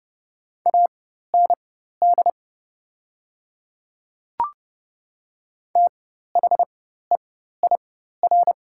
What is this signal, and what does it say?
Morse code, heard as a keyed tone, about 30 wpm, spaced as for 14 wpm, 700 Hz. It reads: AND THEIR
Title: 4x Character Spacing - Top 100 Words - 30wpm